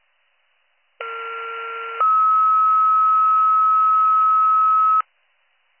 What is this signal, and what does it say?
Two-tone page, heard as a one-second tone, then a three-second tone, about 1370 then 1290 Hz. It is heard from 1.0 to 5.0 s.